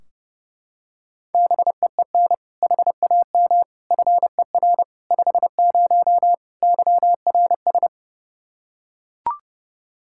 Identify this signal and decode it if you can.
Morse code, heard as a keyed tone, about 30 wpm, 700 Hz. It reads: BEEN HAM FER 50 YRS
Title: Practice QSO - Exchange 3 - More Info - 30wpm